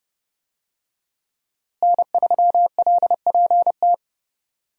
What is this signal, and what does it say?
Morse code, heard as a keyed tone, about 30 wpm, 700 Hz. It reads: N3LPT
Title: USA Call Signs - 5 Characters - Repeated 2x - 30wpm